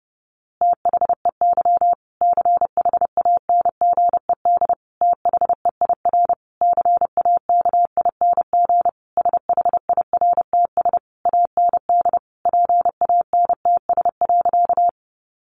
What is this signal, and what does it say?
Morse code, heard as a keyed tone, about 30 wpm, 700 Hz. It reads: THEY CHANGED THEIR CAKING SHIRTS AND PANTS.
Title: A Sound Of Thunder 30wpm